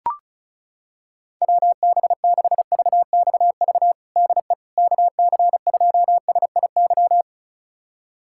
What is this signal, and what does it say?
Morse code, heard as a keyed tone, about 36 wpm, 700 Hz. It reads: WB6VXV DE KC2SIY
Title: USA Callsign DE Callsign 35wpm